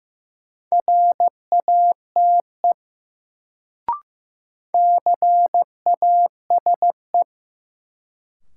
Morse code, heard as a keyed tone, about 15 wpm, 700 Hz. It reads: RATE CASE
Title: Sentences from Top 700 Words - New Word Drill - 15wpm